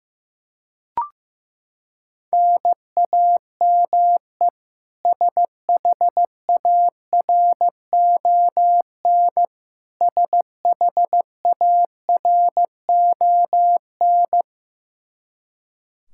Morse code, heard as a keyed tone, about 15 wpm, 700 Hz. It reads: NAME SHARON SHARON
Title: K1USN SST QSO Practice 15wpm